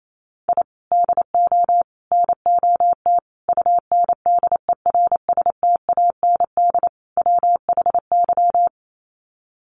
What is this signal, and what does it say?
Morse code, heard as a keyed tone, about 28 wpm, 700 Hz. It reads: I DO NOT UNDERSTAND WHY
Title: Sets of 5 Words Encore 28wpm